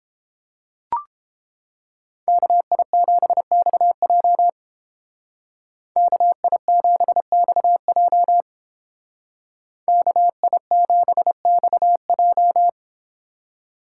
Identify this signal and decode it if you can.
Morse code, heard as a keyed tone, about 28 wpm, 700 Hz. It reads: KI7XJ KI7XJ KI7XJ